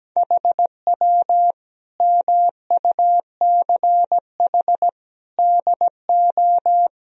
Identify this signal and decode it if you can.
Morse code, heard as a keyed tone, about 17 wpm, 700 Hz. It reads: HW MUCH DO
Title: Sentences from Top 200 Words and Abbreviations 17wpm